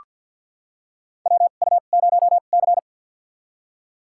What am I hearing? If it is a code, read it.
Morse code, heard as a keyed tone, about 45 wpm, spaced as for 34 wpm, 700 Hz. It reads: WU0C
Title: USA Call Signs - 4 Characters - 2x Character Spacing - 50wpm